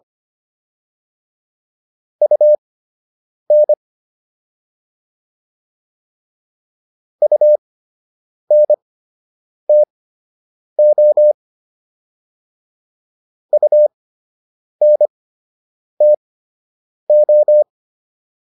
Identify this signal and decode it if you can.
Morse code, heard as a keyed tone, about 25 wpm, spaced as for 8 wpm, 600 Hz. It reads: UN UNTO UNTO